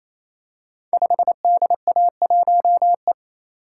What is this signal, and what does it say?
Morse code, heard as a keyed tone, about 28 wpm, 700 Hz. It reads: 5DA1E
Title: Code Groups - 5 Letter:Numbers - 28wpm